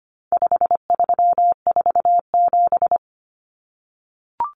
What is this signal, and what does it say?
Morse code, heard as a keyed tone, about 25 wpm, 700 Hz. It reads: 5347
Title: Common Cut Numbers - 4-Digit 25wpm